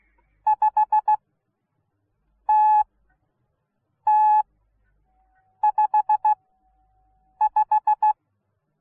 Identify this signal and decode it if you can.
Morse code, heard as a keyed tone, about 16 wpm, spaced as for 6 wpm, 850 Hz. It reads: 5TT55